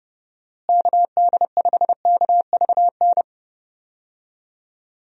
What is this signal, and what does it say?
Morse code, heard as a keyed tone, about 30 wpm, 700 Hz. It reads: KD5KVN
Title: USA Call Signs - 6 Characters - Repeated 3x - 30wpm